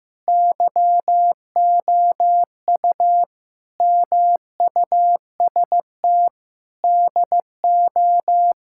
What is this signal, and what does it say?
Morse code, heard as a keyed tone, about 15 wpm, 700 Hz. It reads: YOU MUST DO
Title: Sentences from Top 1000 Words - Review Part 6 - 15wpm